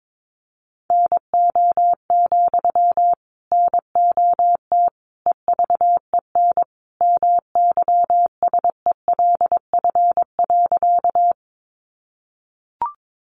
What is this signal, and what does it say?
Morse code, heard as a keyed tone, about 22 wpm, 700 Hz. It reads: NO, NOT EVEN MYSELF.
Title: Sentences from Top 1300 Words 22wpm